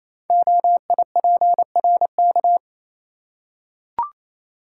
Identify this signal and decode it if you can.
Morse code, heard as a keyed tone, about 27 wpm, 700 Hz. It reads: OIPRK
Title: Code Groups - 5 Letters - 28wpm